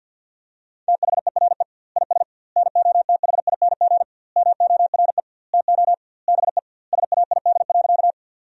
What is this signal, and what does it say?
Morse code, heard as a keyed tone, about 50 wpm, 700 Hz. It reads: THERE IS NOTHING MORE TO BE SAID.
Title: 2x Word-Spacing - Sentences from Top 500 Words - 50wpm